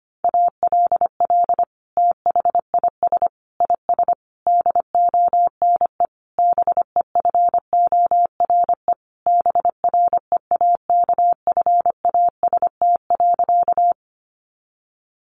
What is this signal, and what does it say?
Morse code, heard as a keyed tone, about 25 wpm, 700 Hz. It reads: ALL THIS IS DONE BEFORE BREAKFAST.
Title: Sentences from Top 1500 Words 25wpm